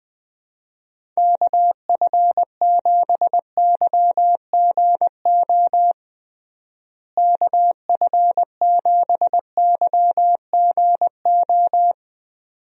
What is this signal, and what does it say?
Morse code, heard as a keyed tone, about 20 wpm, 700 Hz. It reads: KF7YGO KF7YGO
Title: USA Call Signs - 6 Characters - Repeated 2x - 20wpm